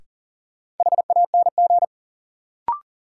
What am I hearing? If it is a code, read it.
Morse code, heard as a keyed tone, about 40 wpm, 700 Hz. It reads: HANG
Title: Sentences from Top 1600 Words - New Word Drill - 40wpm